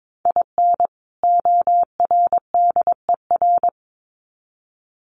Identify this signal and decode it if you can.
Morse code, heard as a keyed tone, about 22 wpm, 700 Hz. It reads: IN ORDER
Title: Sets of 2 Words 22wpm